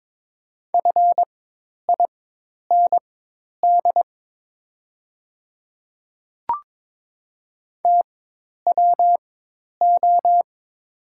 Morse code, heard as a keyed tone, about 22 wpm, spaced as for 10 wpm, 700 Hz. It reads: FIND TWO
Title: 4x Character Spacing - Top 300 Words - 22wpm